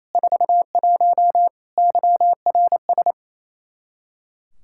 Morse code, heard as a keyed tone, about 28 wpm, 700 Hz. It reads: 41 YRS
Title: K1USN SST QSO Practice 28wpm